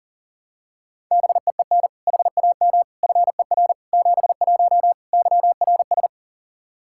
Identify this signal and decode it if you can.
Morse code, heard as a keyed tone, about 37 wpm, 700 Hz. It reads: BEEN HAM FER 71 YRS
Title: Practice QSO - Exchange 3 - More Info - 40wpm